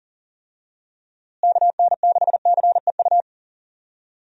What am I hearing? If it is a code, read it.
Morse code, heard as a keyed tone, about 40 wpm, 700 Hz. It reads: KN6CEU
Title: USA Call Signs - 6 Characters - Repeated 2x - 40wpm